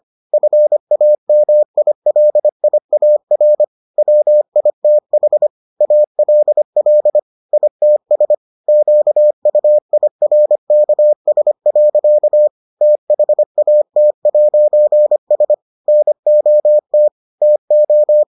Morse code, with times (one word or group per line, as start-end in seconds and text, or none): 0.3-3.6 FAMILIAR
4.0-5.5 WITH
5.8-7.2 ALL
7.5-8.3 ITS
8.7-12.5 QUIRKS.
12.8-15.5 THAT'S
15.9-17.1 NOT
17.4-18.2 TO